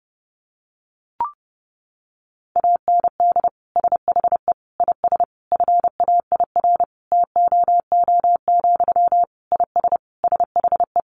Morse code, heard as a keyed tone, about 30 wpm, 700 Hz. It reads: AND SHE IS FAIR TOO, IS SHE
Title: Sentences from Top 1300 Words 30wpm